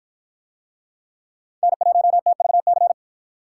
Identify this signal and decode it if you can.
Morse code, heard as a keyed tone, about 53 wpm, 700 Hz. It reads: N1TVC